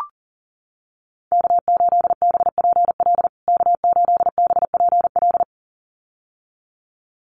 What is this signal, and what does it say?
Morse code, heard as a keyed tone, about 40 wpm, 700 Hz. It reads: K8BPL K8BPL
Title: USA Call Signs Repeated 2x 40wpm